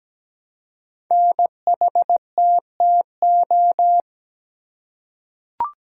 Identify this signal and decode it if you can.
Morse code, heard as a keyed tone, about 17 wpm, 700 Hz. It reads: NHTTO
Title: Code Groups - 5 Letters - 17wpm